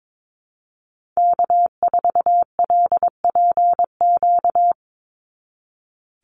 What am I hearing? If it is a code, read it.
Morse code, heard as a keyed tone, about 22 wpm, 700 Hz. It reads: K4LPQ
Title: NA SOTA Activator Callsigns 22wpm